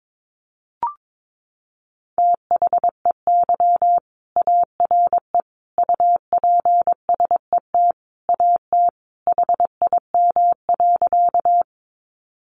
Morse code, heard as a keyed tone, about 22 wpm, 700 Hz. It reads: THEY ARE UPSET AT HIM.
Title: Sentences from Top 800 Words 22wpm